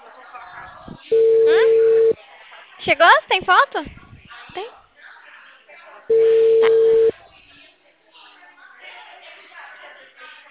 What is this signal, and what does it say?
Ringback tone (China, 450 Hz), with speech.